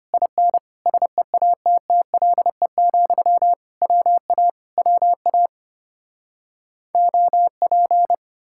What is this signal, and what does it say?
Morse code, heard as a keyed tone, about 33 wpm, 700 Hz. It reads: IN SEATTLE, WA WA OP